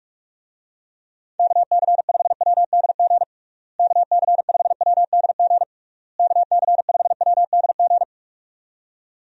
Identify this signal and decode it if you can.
Morse code, heard as a keyed tone, about 46 wpm, 700 Hz. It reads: KC5WDG KC5WDG KC5WDG
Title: USA Call Signs - 6 Characters - Repeated 3x - 45wpm